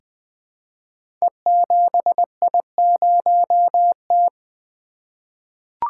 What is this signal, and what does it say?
Morse code, heard as a keyed tone, about 20 wpm, 700 Hz. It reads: E7I0T
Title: Code Groups - 5 Letter:Numbers - 20wpm